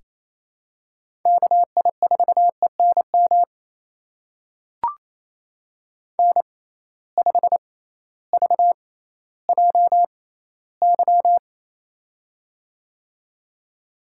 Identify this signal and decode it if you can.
Morse code, heard as a keyed tone, about 27 wpm, spaced as for 15 wpm, 700 Hz. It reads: KI4ENM N5VJY
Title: USA Call Signs - 6x Character Spacing - 28wpm